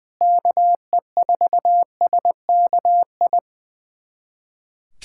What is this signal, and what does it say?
Morse code, heard as a keyed tone, about 20 wpm, 700 Hz. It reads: KE4SKI
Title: USA Callsign DE Callsign - 6 Characters - 20wpm